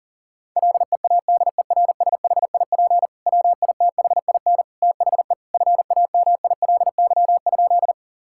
Morse code, heard as a keyed tone, about 40 wpm, 700 Hz. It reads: LEADERSHIP WITHIN THE FAMILY?
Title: Sentences from Top 1600 Words 40wpm